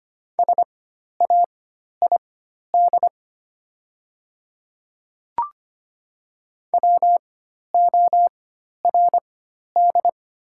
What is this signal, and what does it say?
Morse code, heard as a keyed tone, about 25 wpm, spaced as for 12 wpm, 700 Hz. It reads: SAID WORD
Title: 4x Character Spacing - Top 200 Words - 25wpm